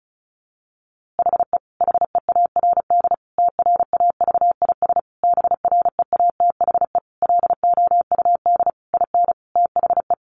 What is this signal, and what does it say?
Morse code, heard as a keyed tone, about 34 wpm, 700 Hz. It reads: HE HEARD TRAVIS BREATHE LOUD IN THE
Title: A Sound Of Thunder 35wpm